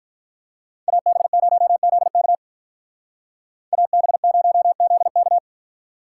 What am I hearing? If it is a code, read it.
Morse code, heard as a keyed tone, about 42 wpm, 700 Hz. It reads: AB0ZK AB0ZK
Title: Speed Racing - USA 2x2 Call Signs 35wpm